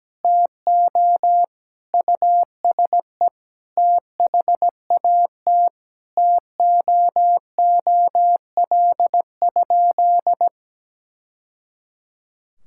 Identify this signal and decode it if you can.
Morse code, heard as a keyed tone, about 17 wpm, 700 Hz. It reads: TO USE THAT TOOL?
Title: Sentences from Top 1000 Words - Review Part 6 - 17wpm